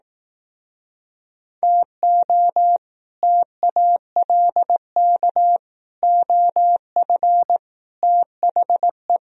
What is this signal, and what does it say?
Morse code, heard as a keyed tone, about 18 wpm, 700 Hz. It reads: TO TALK OF THE
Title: Speed-Racing - Sets of 5 Words 15wpm